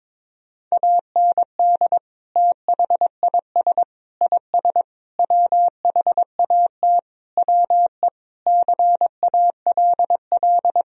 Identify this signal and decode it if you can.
Morse code, heard as a keyed tone, about 22 wpm, 700 Hz. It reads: AND THIS IS WHAT WE CALL
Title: Sentences from Top 1000 Words - Review Part 3 - 22wpm